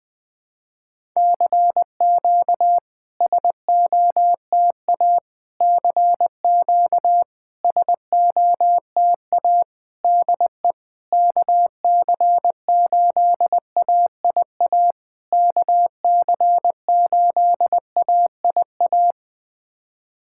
Morse code, heard as a keyed tone, about 21 wpm, 690 Hz. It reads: CQ SOTA CQ SOTA DE KC8AIA KC8AIA
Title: SOTA QSO Practice 20wpm